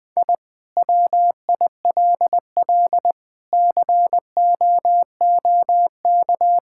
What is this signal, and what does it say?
Morse code, heard as a keyed tone, about 20 wpm, 700 Hz. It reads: I WILL COOK
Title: Sentences from Top 1000 Words - Review Part 6 - 20wpm